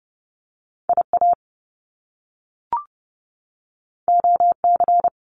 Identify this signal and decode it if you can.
Morse code, heard as a keyed tone, about 30 wpm, 700 Hz. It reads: IA OC
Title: Top Two Letter Combinations in Words 30wpm